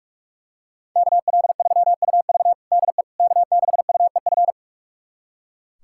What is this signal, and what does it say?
Morse code, heard as a keyed tone, about 42 wpm, 700 Hz. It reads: KL3UV DE K6UEF
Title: USA Callsign DE Callsign - 5 Characters - 45wpm